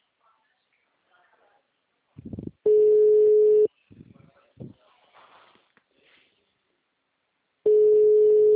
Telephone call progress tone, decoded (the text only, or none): ringback tone